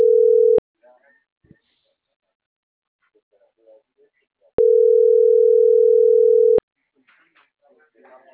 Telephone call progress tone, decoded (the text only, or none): ringback tone